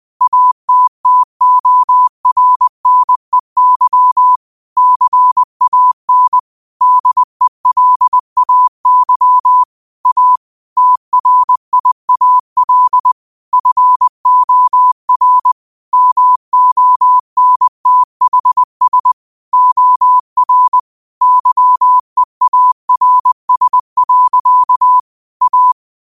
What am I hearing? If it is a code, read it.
Morse code, heard as a keyed tone, about 20 wpm, 1000 Hz. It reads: ATTORNEY CAN DELAY A TRIAL FOR MONTHS OR YEARS. A
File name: qotdfiles / 20wpm